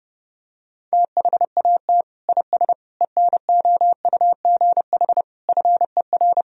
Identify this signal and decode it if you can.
Morse code, heard as a keyed tone, about 32 wpm, 700 Hz. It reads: THAT IS ENOUGH FER
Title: Sentences from Top 400 Words and Abbreviations 30wpm